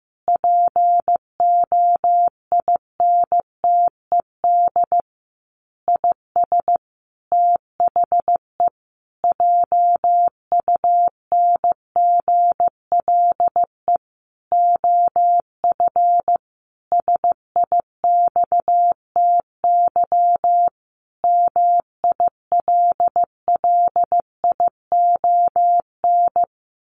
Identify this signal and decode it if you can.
Morse code, heard as a keyed tone, about 15 wpm, 700 Hz. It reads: POINTED IS THE JUNGLE OF SIXTY MILLION
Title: A Sound Of Thunder 15wpm